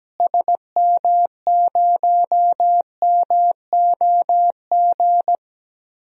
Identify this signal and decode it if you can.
Morse code, heard as a keyed tone, about 17 wpm, 700 Hz. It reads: SM0MOG
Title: DXCC-100 Call Signs - 6 Characters - Repeated 2x - 17wpm